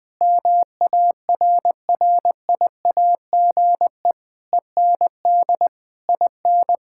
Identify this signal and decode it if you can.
Morse code, heard as a keyed tone, about 20 wpm, 700 Hz. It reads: MARRIAGE END IN
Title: Sentences from Top 1000 Words - Review Part 6 - 20wpm